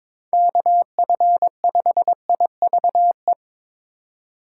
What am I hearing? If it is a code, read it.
Morse code, heard as a keyed tone, about 23 wpm, 700 Hz. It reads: KF5IVE
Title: USA Call Signs - 3x Character Spacing - 22wpm